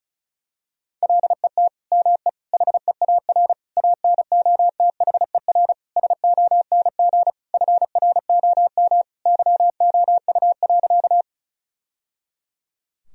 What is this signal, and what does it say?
Morse code, heard as a keyed tone, about 36 wpm, 700 Hz. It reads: LET ME HEAR ANOTHER SONG FROM YOU.
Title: Sentences from Top 1000 Words - Review Part 2 - 35wpm